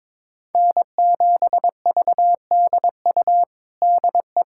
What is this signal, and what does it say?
Morse code, heard as a keyed tone, about 22 wpm, 700 Hz. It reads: N7VDU DE
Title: USA Callsign DE Callsign - 5 Characters - 22wpm